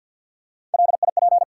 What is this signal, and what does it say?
Morse code, heard as a keyed tone, about 48 wpm, 700 Hz. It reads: LIP
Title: Three Letter Words - 50wpm